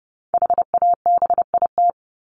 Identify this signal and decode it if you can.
Morse code, heard as a keyed tone, about 30 wpm, 700 Hz. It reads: HABIT
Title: Top 1751-2000 Words 30wpm